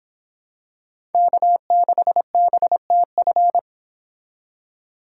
Morse code, heard as a keyed tone, about 26 wpm, 700 Hz. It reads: K6BTF